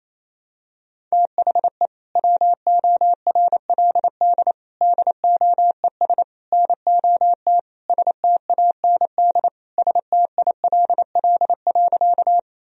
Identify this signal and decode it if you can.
Morse code, heard as a keyed tone, about 28 wpm, 700 Hz. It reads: THE WORLD DOES NOT STAND STILL.
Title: Sentences from Top 1000 Words - Review Part 1 - 28wpm